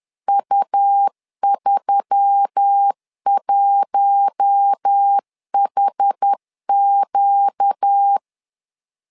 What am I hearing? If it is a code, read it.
Morse code, heard as a keyed tone, about 10 wpm, 800 Hz. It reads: U31HQ
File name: LEZIONI / 4 Q1S3 10wpm 2427